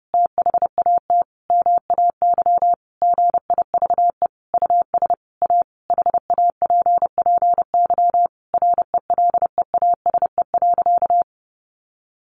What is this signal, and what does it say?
Morse code, heard as a keyed tone, about 30 wpm, 700 Hz. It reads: THAT MAY GIVE US A HAPPY RELEASE.
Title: Sentences from Top 1100 Words 30wpm